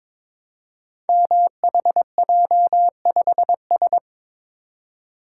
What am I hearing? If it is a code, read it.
Morse code, heard as a keyed tone, about 22 wpm, 700 Hz. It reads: MHJ5S
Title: Code Groups - 5 Letter:Numbers - 22wpm